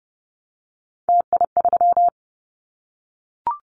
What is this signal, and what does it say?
Morse code, heard as a keyed tone, about 30 wpm, 700 Hz. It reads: TI3
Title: Callsign Prefixes for Top 100 DXCC - 30wpm